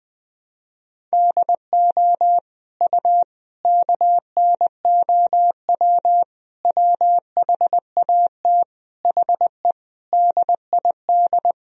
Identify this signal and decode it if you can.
Morse code, heard as a keyed tone, about 20 wpm, 700 Hz. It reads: DO U KNOW WHAT HE DID
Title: Sentences from Top 300 Words and Abbreviations 20wpm